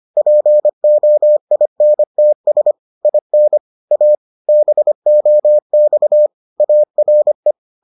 Morse code, heard as a keyed tone, about 25 wpm, 600 Hz. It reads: POINTS IN A BOX ARE